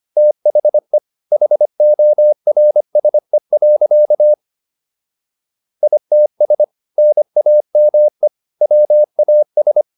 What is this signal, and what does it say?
Morse code, heard as a keyed tone, about 25 wpm, 600 Hz. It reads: THE HORSE. ITS NAME WAS
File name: SS123 25